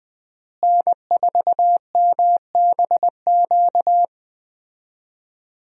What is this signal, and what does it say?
Morse code, heard as a keyed tone, about 21 wpm, 700 Hz. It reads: N4MBQ
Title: USA Call Signs - 5 Characters - Repeated 3x - 20wpm